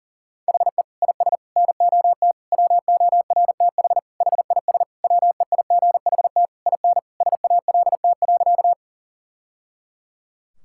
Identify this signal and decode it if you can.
Morse code, heard as a keyed tone, about 40 wpm, 700 Hz. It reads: HE IS NOT WORTH HIS WEIGHT IN SALT.
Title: Sentences from Top 1000 Words - Review Part 5 - 40wpm